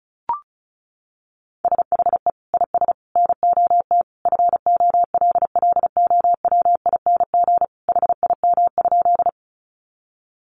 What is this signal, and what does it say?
Morse code, heard as a keyed tone, about 34 wpm, 700 Hz. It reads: SHE IS NOT FOLLOWING HIM?
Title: Sentences from Top 700 Words 35wpm